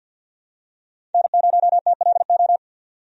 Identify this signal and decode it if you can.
Morse code, heard as a keyed tone, about 52 wpm, 700 Hz. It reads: N0TLO
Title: USA Call Signs - 5 Characters - 50wpm